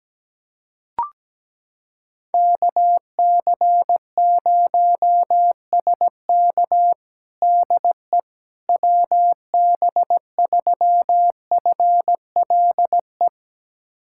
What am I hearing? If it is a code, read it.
Morse code, heard as a keyed tone, about 17 wpm, 700 Hz. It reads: KC0SK DE WB3FLE
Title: USA Callsign DE Callsign 17wpm